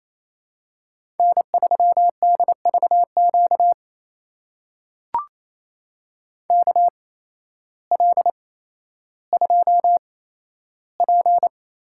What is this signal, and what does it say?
Morse code, heard as a keyed tone, about 28 wpm, spaced as for 13 wpm, 700 Hz. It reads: N3DVQ KL2P
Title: USA Call Signs - 5 Characters - 8x Character Spacing - 28wpm